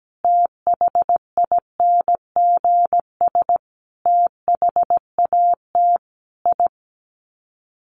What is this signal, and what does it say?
Morse code, heard as a keyed tone, about 17 wpm, 700 Hz. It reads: THINGS THAT I
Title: Sets of 5 Words Encore 17wpm